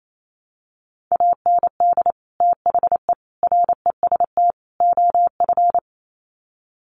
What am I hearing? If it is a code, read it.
Morse code, heard as a keyed tone, about 28 wpm, 700 Hz. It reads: AND THE REST OF
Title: Sets of 4 Words 28wpm